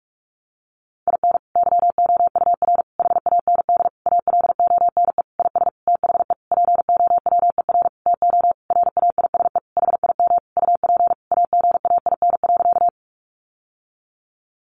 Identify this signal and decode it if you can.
Morse code, heard as a keyed tone, about 44 wpm, 700 Hz. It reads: IN YOUR HAND ALONE IS THE POWER TO RAISE HIM UP AGAIN.
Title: Sentences from Top 1400 Words 45wpm